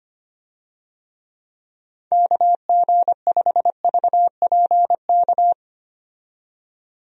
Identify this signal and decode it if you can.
Morse code, heard as a keyed tone, about 25 wpm, 700 Hz. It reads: KG5VPK
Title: USA Call Signs - 6 Characters - 25wpm